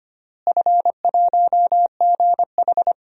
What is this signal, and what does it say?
Morse code, heard as a keyed tone, about 25 wpm, 700 Hz. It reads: F1GH
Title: DXCC-100 Call Signs - 4 Characters - Repeated 2x - 25wpm